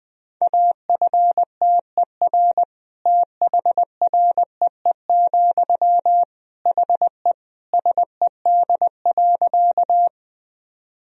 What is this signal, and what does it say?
Morse code, heard as a keyed tone, about 20 wpm, 700 Hz. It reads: AFTER THREE, HE SED.
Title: Sentences from Top 200 Words and Abbreviations 20wpm